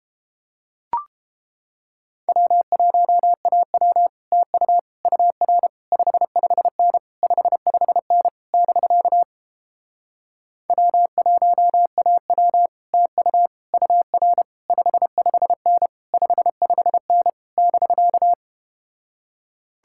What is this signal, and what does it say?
Morse code, heard as a keyed tone, about 32 wpm, 700 Hz. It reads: W1AW TU UR 55N 55N <BK> W1AW TU UR 55N 55N <BK>